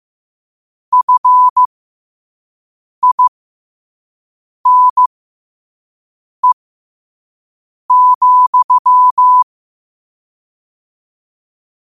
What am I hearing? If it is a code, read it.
Morse code, heard as a keyed tone, about 15 wpm, spaced as for 5 wpm, 1000 Hz. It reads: FINE,
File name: qotdfiles / 5wpm